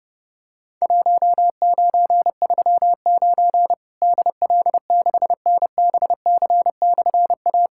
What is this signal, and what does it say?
Morse code, heard as a keyed tone, about 30 wpm, 700 Hz. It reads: 1939 DL6NBC/A
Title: Worked All Europe DX Contest - QTC Exchange - 30wpm